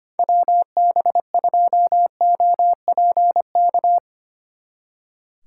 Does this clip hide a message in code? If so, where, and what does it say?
Yes: Morse code from 0.2 to 4.0 s, heard as a keyed tone, about 32 wpm, 700 Hz. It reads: WB2OPK